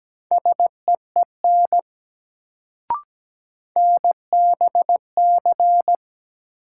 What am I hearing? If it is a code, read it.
Morse code, heard as a keyed tone, about 17 wpm, 700 Hz. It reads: SEEN NBC
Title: Sets of 5 Words Encore - New Word Drill - 17wpm